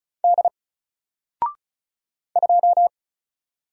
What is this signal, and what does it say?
Morse code, heard as a keyed tone, about 35 wpm, 700 Hz. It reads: D 2
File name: Single-Letter-Number-35wpm